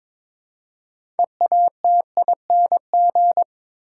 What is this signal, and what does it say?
Morse code, heard as a keyed tone, about 22 wpm, 700 Hz. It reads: EATING
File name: Difficult Sentences with Hints - 22wpm